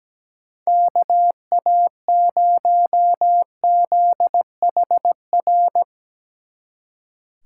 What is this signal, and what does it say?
Morse code, heard as a keyed tone, about 18 wpm, 700 Hz. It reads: KA0ZHR
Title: Speed Racing - USA Call Signs 17wpm